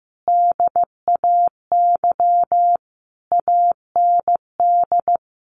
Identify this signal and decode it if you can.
Morse code, heard as a keyed tone, about 15 wpm, 700 Hz. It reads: DAY AND
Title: A Sound Of Thunder 15wpm